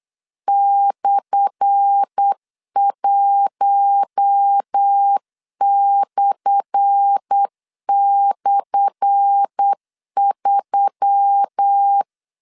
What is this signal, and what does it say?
Morse code, heard as a keyed tone, about 8 wpm, 800 Hz. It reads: /1//3
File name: LEZIONI / 10 SIGNS / 10A SLASH 8wpm 3452